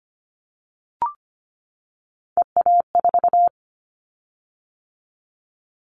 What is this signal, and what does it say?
Morse code, heard as a keyed tone, about 25 wpm, 700 Hz. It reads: EA4
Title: Callsign Prefixes for Top 25 DXCC - 25wpm